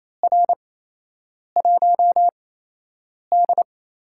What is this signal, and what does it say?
Morse code, heard as a keyed tone, about 28 wpm, spaced as for 8 wpm, 700 Hz. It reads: R1D
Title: USA Call Signs - 4 Characters - 8x Character Spacing - 28wpm